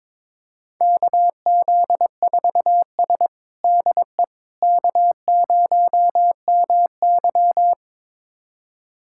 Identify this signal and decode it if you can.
Morse code, heard as a keyed tone, about 21 wpm, 700 Hz. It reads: KZ4S DE K0MY
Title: USA Callsign DE Callsign - 4 Characters - 22wpm